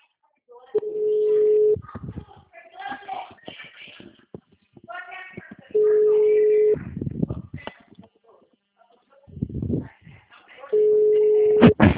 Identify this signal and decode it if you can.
Ringback tone (Europe, 425 Hz), with speech.
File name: out- -2012- - - .9910